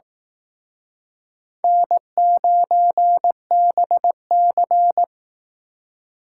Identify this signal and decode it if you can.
Morse code, heard as a keyed tone, about 19 wpm, 700 Hz. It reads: N9BC